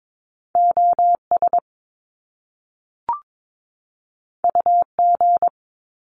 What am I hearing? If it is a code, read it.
Morse code, heard as a keyed tone, about 22 wpm, 700 Hz. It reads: OS UG
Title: Top Two Letter Combinations in Words 22wpm